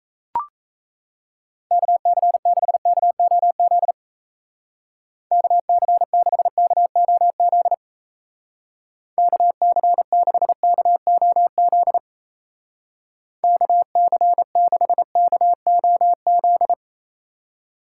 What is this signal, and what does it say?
Morse code, heard as a keyed tone, about 34 wpm, 700 Hz. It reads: KC6KOZ KC6KOZ KC6KOZ KC6KOZ
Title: Speed Racing - USA Call Signs 28wpm